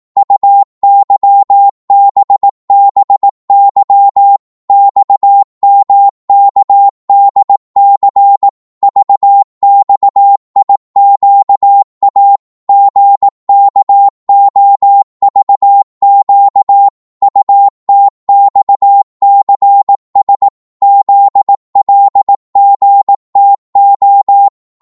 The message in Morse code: UYBBY XMKDC VXIQA GKOVQ UTXCS ZLGTO